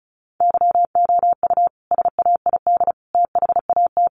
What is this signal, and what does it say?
Morse code, heard as a keyed tone, about 34 wpm, 700 Hz. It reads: YOU SAID THAT
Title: Sentences from Top 1300 Words 35wpm